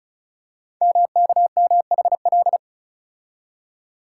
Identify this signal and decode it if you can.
Morse code, heard as a keyed tone, about 36 wpm, 700 Hz. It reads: MKMHL
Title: Code Groups - 5 Letters - 35wpm